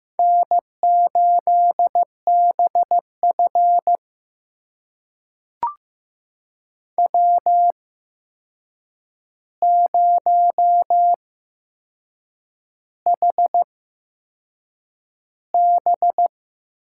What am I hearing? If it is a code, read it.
Morse code, heard as a keyed tone, about 15 wpm, spaced as for 6 wpm, 700 Hz. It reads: N8BF W0HB
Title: USA Call Signs - 4 Characters - 8x Character Spacing - 15wpm